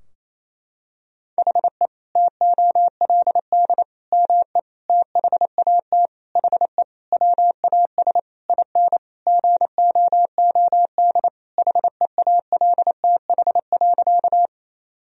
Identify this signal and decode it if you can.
Morse code, heard as a keyed tone, about 28 wpm, 700 Hz. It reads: HE TOLD ME THAT HE WAS IN GOOD HEALTH.
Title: Sentences from Top 1000 Words - Review Part 3 - 28wpm